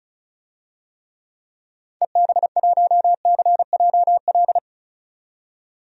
Morse code, heard as a keyed tone, about 36 wpm, 700 Hz. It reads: EB1CJL
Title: DXCC-100 Call Signs - 6 Characters - 35wpm